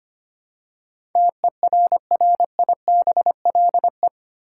Morse code, heard as a keyed tone, about 25 wpm, 700 Hz. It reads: TERRIBLE
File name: Eight Letter Words - 25wpm